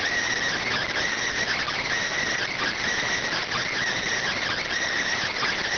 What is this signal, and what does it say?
Emergency Alert System (SAME) tones, heard as harsh buzzing bursts. It is heard from 0.6 to 4.6 s.